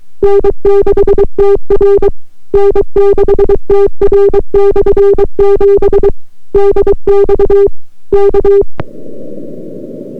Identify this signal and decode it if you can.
Morse code, heard as a keyed tone, about 23 wpm, 400 Hz. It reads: N6TR N6TR/7 DX K